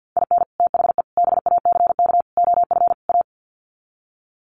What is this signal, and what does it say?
Morse code, heard as a keyed tone, about 48 wpm, 700 Hz. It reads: IN THE BACK OF A
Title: Sets of 5 Words Encore2 50wpm